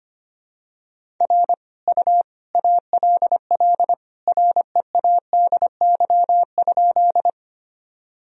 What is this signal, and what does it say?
Morse code, heard as a keyed tone, about 25 wpm, 700 Hz. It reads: R U ALL READY?
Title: Sentences from Top 400 Words and Abbreviations 25wpm